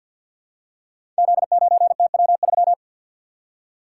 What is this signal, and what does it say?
Morse code, heard as a keyed tone, about 50 wpm, 700 Hz. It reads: Z9TW3
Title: Code Groups - 5 Letter:Numbers - 50wpm